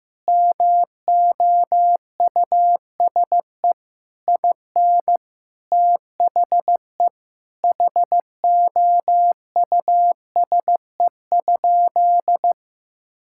Morse code, heard as a keyed tone, about 15 wpm, 700 Hz. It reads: MOUSE IN THE HOUSE?
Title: Sentences from Top 1000 Words - Review Part 5 - 15wpm